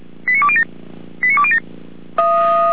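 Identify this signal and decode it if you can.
Five-tone selective call (ZVEI), heard as quick runs of five tones, tones of about 70 ms. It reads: 89207 89207
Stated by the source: ZVEI five-tone selective call 89187 received by SDR.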